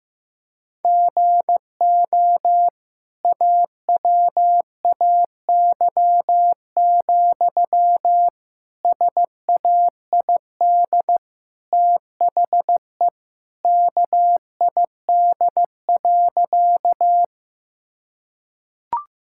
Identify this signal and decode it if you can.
Morse code, heard as a keyed tone, about 16 wpm, 700 Hz. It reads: GO AWAY, SAID THE KID.
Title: Sentences from Top 1600 Words 15wpm